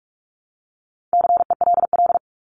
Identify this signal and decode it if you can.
Morse code, heard as a keyed tone, about 44 wpm, 700 Hz. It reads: CELL
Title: Top 1001-1250 Words 45wpm